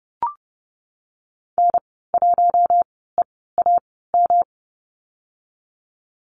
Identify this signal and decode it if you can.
Morse code, heard as a keyed tone, about 30 wpm, 700 Hz. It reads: N 1 E A M
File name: Lesson 19g - Letter w - Call Signs 30wpm